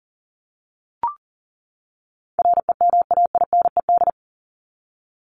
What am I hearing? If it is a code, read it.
Morse code, heard as a keyed tone, about 40 wpm, 700 Hz. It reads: REMAINED
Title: ED Suffix - Top 100 Words with Base Words 40wpm